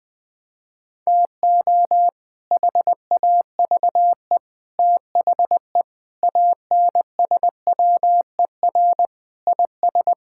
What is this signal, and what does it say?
Morse code, heard as a keyed tone, about 20 wpm, 700 Hz. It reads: TO HAVE THE ANSWER IS
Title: Sentences with 3-Word Phrases - Spoken Beginning - 20wpm